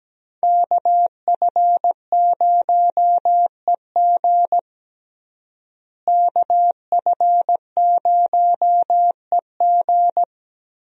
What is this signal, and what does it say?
Morse code, heard as a keyed tone, about 17 wpm, 700 Hz. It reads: KF0EG KF0EG
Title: USA Call Signs - 5 Characters - Repeated 2x - 17wpm